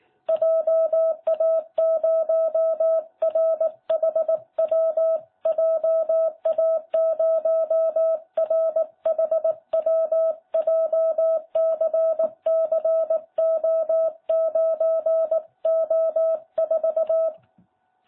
Morse code, heard as a keyed tone, about 18 wpm, 660 Hz. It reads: JA0RHW JA0RHW JCCO9O4